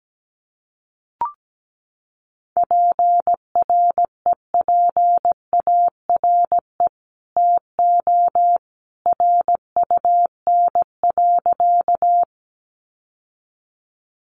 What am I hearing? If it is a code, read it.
Morse code, heard as a keyed tone, about 17 wpm, 700 Hz. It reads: PREPARE TO RUN.
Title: Sentences from Top 900 Words 17wpm